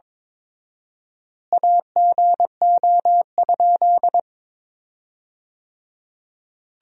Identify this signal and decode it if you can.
Morse code, heard as a keyed tone, about 22 wpm, 700 Hz. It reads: AGO?
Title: 4x Word-Spacing - Sentences from Top 500 Words - 22wpm